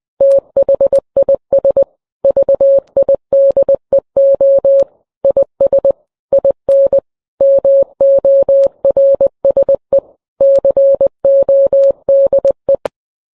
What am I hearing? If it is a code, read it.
Morse code, heard as a keyed tone, about 20 wpm, 550 Hz. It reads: THIS VIDEO IS IN MORSE CODE